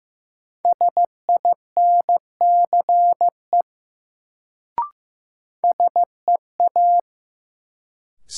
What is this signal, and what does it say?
Morse code, heard as a keyed tone, about 15 wpm, 700 Hz. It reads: SINCE SEA
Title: Sentences from Top 300 Words - New Word Drill - 15wpm